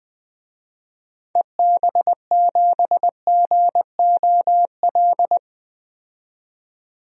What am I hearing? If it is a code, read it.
Morse code, heard as a keyed tone, about 20 wpm, 700 Hz. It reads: EB7GOL
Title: DXCC-100 Call Signs - 6 Characters - 20wpm